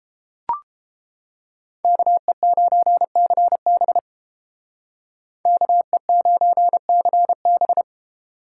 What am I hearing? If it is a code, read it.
Morse code, heard as a keyed tone, about 31 wpm, 700 Hz. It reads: KE9CB KE9CB